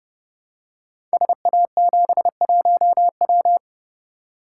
Morse code, heard as a keyed tone, about 30 wpm, 700 Hz. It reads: SA71W
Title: Code Groups - 5 Letter:Numbers - 30wpm